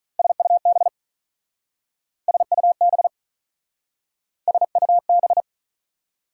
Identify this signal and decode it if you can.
Morse code, heard as a keyed tone, about 47 wpm, 700 Hz. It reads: SUB SUB SUB